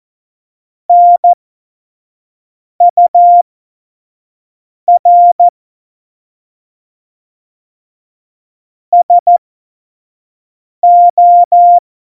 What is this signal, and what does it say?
Morse code, heard as a keyed tone, about 14 wpm, spaced as for 5 wpm, 700 Hz. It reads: NUR SO
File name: Morsekurs / Witze / Witz 15. Januar 2019, 25 F70